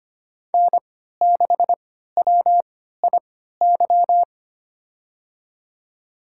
Morse code, heard as a keyed tone, about 25 wpm, spaced as for 14 wpm, 700 Hz. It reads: N6WIY
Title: USA Call Signs - 5 Characters - 3x Character Spacing - 25wpm